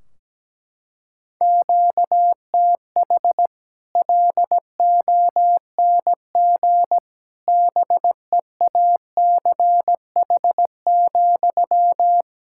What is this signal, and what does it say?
Morse code, heard as a keyed tone, about 17 wpm, 700 Hz. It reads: QTH LONG BEACH,
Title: Practice QSO - Exchange 1 - Beginning - 17wpm